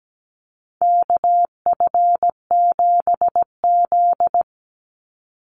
Morse code, heard as a keyed tone, about 17 wpm, 700 Hz. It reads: KF7Z
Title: Call Signs Repeated 3x - 17wpm